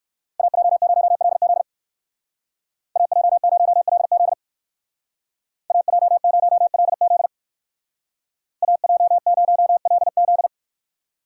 Morse code, heard as a keyed tone, about 59 wpm, 700 Hz. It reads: AJ0LZ AJ0LZ AJ0LZ AJ0LZ